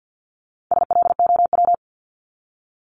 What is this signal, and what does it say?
Morse code, heard as a keyed tone, about 50 wpm, 700 Hz. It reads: SLOW
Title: LY Suffix - Top 100 Words with Base Words 50wpm